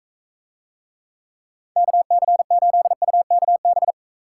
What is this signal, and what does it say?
Morse code, heard as a keyed tone, about 45 wpm, 700 Hz. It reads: KC8UKB